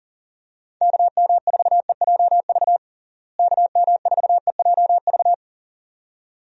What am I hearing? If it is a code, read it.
Morse code, heard as a keyed tone, about 40 wpm, 700 Hz. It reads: KM4EJV KM4EJV
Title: USA Call Signs - 6 Characters - Repeated 2x - 40wpm